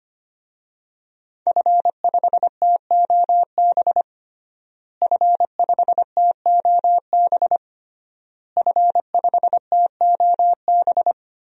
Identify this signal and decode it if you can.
Morse code, heard as a keyed tone, about 25 wpm, 700 Hz. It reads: F5TOB F5TOB F5TOB
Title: DXCC-100 Call Signs - 5 Characters - Repeated 3x - 25wpm